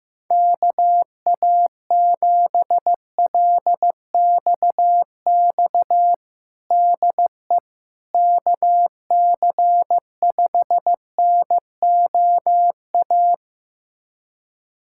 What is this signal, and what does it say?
Morse code, heard as a keyed tone, about 15 wpm, 700 Hz. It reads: KA7LXX DE KC5NOA
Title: USA Callsign DE Callsign 15wpm